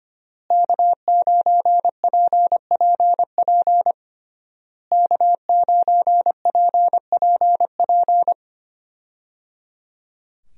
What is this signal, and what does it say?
Morse code, heard as a keyed tone, about 25 wpm, 700 Hz. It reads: K9PPP K9PPP
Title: USA Call Signs - 5 Characters - Repeated 3x - 25wpm